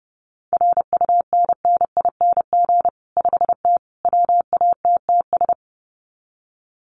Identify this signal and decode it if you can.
Morse code, heard as a keyed tone, about 30 wpm, 700 Hz. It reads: RUNNING 5T WATTS
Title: Common QSO Phrases 30wpm